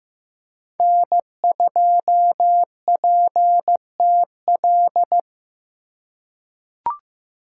Morse code, heard as a keyed tone, about 15 wpm, 700 Hz. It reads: N2PTL
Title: USA Call Signs - 4x Character Spacing - 15wpm